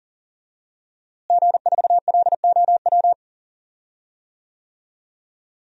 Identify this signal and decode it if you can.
Morse code, heard as a keyed tone, about 37 wpm, 700 Hz. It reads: G4LOW
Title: DXCC-100 Call Signs - 5 Characters - Repeated 2x - 40wpm